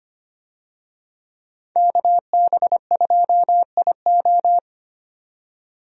Morse code, heard as a keyed tone, about 25 wpm, 700 Hz. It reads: KB2IO
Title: USA Call Signs - 5 Characters - 25wpm